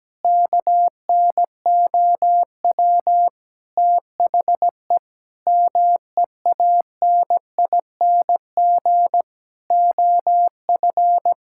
Morse code, heard as a keyed tone, about 17 wpm, 700 Hz. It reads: KNOW THE MEANING OF
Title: Sentences from Top 1000 Words - Review Part 6 - 17wpm